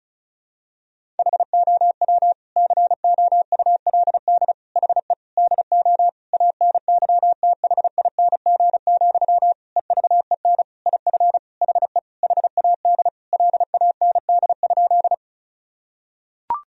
Morse code, heard as a keyed tone, about 34 wpm, 700 Hz. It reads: HOW COULD HE DO ANYTHING, EVEN IF HE HAD LAND?
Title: Sentences from Top 1000 Words - Review Part 1 - 35wpm